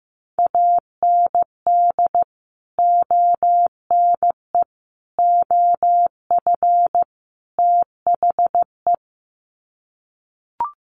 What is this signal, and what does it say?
Morse code, heard as a keyed tone, about 15 wpm, 700 Hz. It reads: AND ONE OF THE
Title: Sets of 4 Words 15wpm